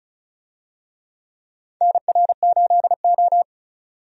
Morse code, heard as a keyed tone, about 36 wpm, 700 Hz. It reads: NR8O
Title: USA Call Signs - 4 Characters - 35wpm